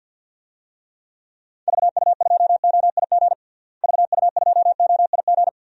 Morse code, heard as a keyed tone, about 46 wpm, 700 Hz. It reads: VU2OIG VU2OIG
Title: DXCC-100 Call Signs - 6 Characters - Repeated 2x - 50wpm